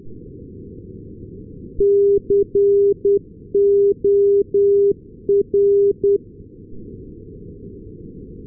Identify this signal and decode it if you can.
Morse code, heard as a keyed tone, about 10 wpm, 400 Hz. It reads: COR
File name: ndb corner cor-388